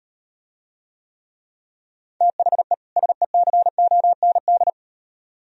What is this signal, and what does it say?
Morse code, heard as a keyed tone, about 38 wpm, 700 Hz. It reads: THE SECOND